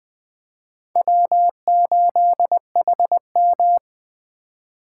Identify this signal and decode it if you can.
Morse code, heard as a keyed tone, about 22 wpm, 700 Hz. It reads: W8HM